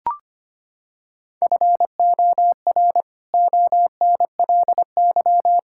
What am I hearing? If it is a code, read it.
Morse code, heard as a keyed tone, about 25 wpm, 700 Hz. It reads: FOR ONLY
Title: Pangram Challenge v2 - 25wpm